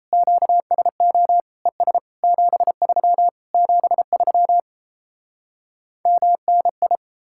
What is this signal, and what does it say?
Morse code, heard as a keyed tone, about 38 wpm, 700 Hz. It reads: QSO ES 73 73 MNI